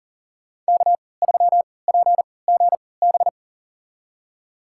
Morse code, heard as a keyed tone, about 40 wpm, spaced as for 23 wpm, 700 Hz. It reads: K3PGB
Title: USA Call Signs - 5 Characters - 3x Character Spacing - 40wpm